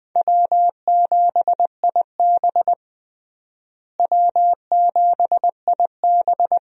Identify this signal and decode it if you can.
Morse code, heard as a keyed tone, about 20 wpm, 700 Hz. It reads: W7IB W7IB
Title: USA Call Signs - 4 Characters - Repeated 2x - 20wpm